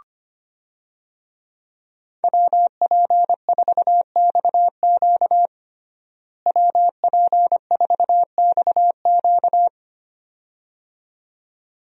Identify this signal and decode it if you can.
Morse code, heard as a keyed tone, about 25 wpm, 700 Hz. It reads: WP4XQ WP4XQ
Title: USA Call Signs - 5 Characters - Repeated 2x - 25wpm